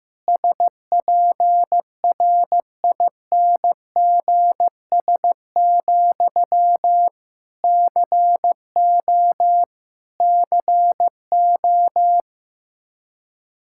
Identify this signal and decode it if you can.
Morse code, heard as a keyed tone, about 16 wpm, 700 Hz. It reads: SPRINGS, CO CO
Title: K1USN SST QSO Practice 15wpm